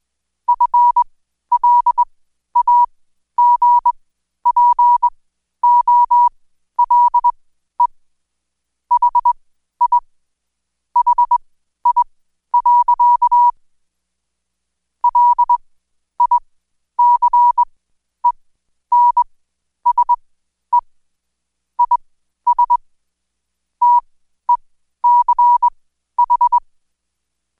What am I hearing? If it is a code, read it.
Morse code, heard as a keyed tone, about 21 wpm, spaced as for 11 wpm, 970 Hz. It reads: FLAGPOLE HI HI. LICENSE IS TECH
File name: Code Quick / AudioDownld2 / Seg121-Day26-DayDrill 26